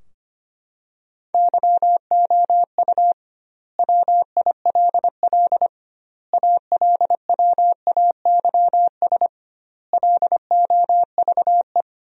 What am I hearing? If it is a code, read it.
Morse code, heard as a keyed tone, about 25 wpm, 700 Hz. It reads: YOU WILL ALWAYS LOVE
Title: 2x Word-Spacing - Sentences from Top 400 Words - 25wpm